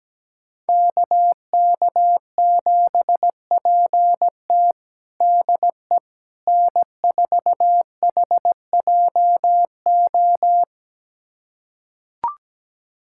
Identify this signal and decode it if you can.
Morse code, heard as a keyed tone, about 17 wpm, 700 Hz. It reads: KK7PT DE N4HJO
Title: USA Callsign DE Callsign - 5 Characters - 17wpm